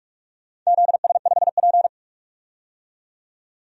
Morse code, heard as a keyed tone, about 46 wpm, 700 Hz. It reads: ZS5P
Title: DXCC-100 Call Signs - 4 Characters - 45wpm